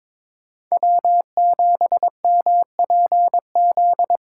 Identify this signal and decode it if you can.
Morse code, heard as a keyed tone, about 21 wpm, 700 Hz. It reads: W7MPZ
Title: USA Call Signs - 5 Characters - 4x Character Spacing - 22wpm